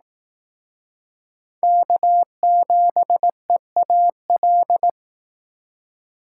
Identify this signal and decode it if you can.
Morse code, heard as a keyed tone, about 18 wpm, 700 Hz. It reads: K7EAL
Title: Speed Racing - USA 1x3 Call Signs 15wpm